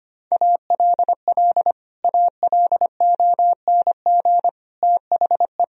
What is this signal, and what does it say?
Morse code, heard as a keyed tone, about 25 wpm, 700 Hz. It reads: ALL ALONG THE
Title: Sentences from Top 1000 Words - Review Part 2 - 25wpm